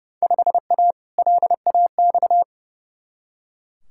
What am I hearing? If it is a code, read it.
Morse code, heard as a keyed tone, about 30 wpm, 700 Hz. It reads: 5A LAX
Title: Field Day Exchanges - 30wpm